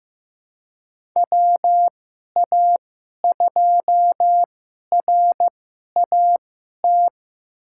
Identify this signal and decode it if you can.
Morse code, heard as a keyed tone, about 15 wpm, spaced as for 11 wpm, 700 Hz. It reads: WA2RAT
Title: USA Call Signs - 2x Character Spacing - 15wpm